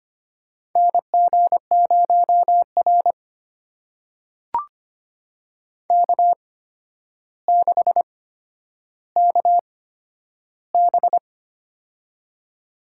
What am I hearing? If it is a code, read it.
Morse code, heard as a keyed tone, about 25 wpm, spaced as for 11 wpm, 700 Hz. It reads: NG0R K6KB
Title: USA Call Signs - 8x Character Spacing - 25wpm